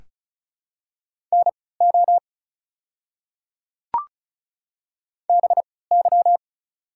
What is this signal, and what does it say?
Morse code, heard as a keyed tone, about 34 wpm, spaced as for 20 wpm, 690 Hz. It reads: NO BY
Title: 3x Character Spacing - Top 100 Words - 35wpm